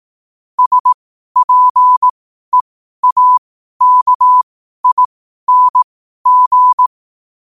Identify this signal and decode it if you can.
Morse code, heard as a keyed tone, about 18 wpm, spaced as for 12 wpm, 1000 Hz. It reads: SPEAKING
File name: qotdfiles / 12wpm